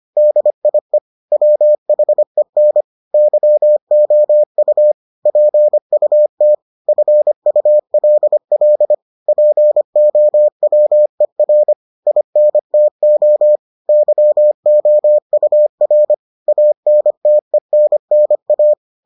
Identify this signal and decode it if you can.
Morse code, heard as a keyed tone, about 25 wpm, 600 Hz. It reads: DIE WHEN YOU PUT FULL POWER INTO YOUR ANTENNA